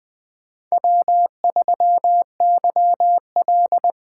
Morse code, heard as a keyed tone, about 24 wpm, 700 Hz. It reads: W3YL
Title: Speed Racing - USA 1x2 Call Signs 20wpm